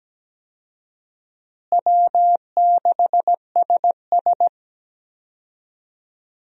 Morse code, heard as a keyed tone, about 17 wpm, 700 Hz. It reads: W6SS
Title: USA Call Signs - 4 Characters - Repeated 2x - 17wpm